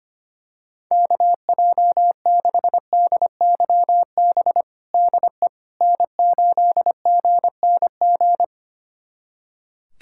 Morse code, heard as a keyed tone, about 25 wpm, 700 Hz. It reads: KJ6DYB DE N8GNG
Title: USA Callsign DE Callsign 25wpm